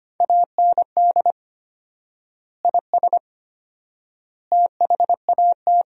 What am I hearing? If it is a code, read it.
Morse code, heard as a keyed tone, about 25 wpm, 700 Hz. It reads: AND IS THAT
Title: 4x Word-Spacing - Sentences from Top 100 Words - 25wpm